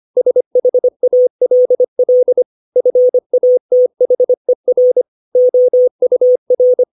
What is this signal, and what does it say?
Morse code, heard as a keyed tone, about 25 wpm, 500 Hz. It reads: SHALL FATHER OUR
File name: 632 words / 632 Words 25WPM